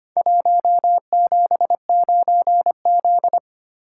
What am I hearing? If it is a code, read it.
Morse code, heard as a keyed tone, about 25 wpm, 700 Hz. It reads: 179Z
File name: practice / 25wpm / Four Letter-Number Combinations - 25wpm